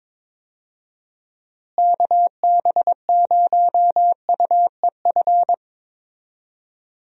Morse code, heard as a keyed tone, about 22 wpm, 700 Hz. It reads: KB0UEF
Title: USA Call Signs - 6 Characters - Repeated 2x - 22wpm